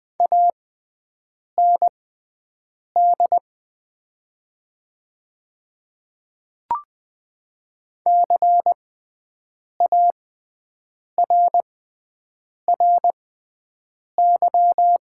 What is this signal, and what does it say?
Morse code, heard as a keyed tone, about 20 wpm, spaced as for 7 wpm, 700 Hz. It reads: AND CARRY
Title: 6x Character Spacing - Top 300 Words - 20wpm